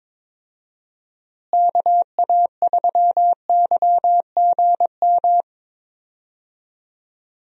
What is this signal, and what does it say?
Morse code, heard as a keyed tone, about 22 wpm, 700 Hz. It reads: KA3YGM
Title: USA Call Signs - 6 Characters - Repeated 3x - 22wpm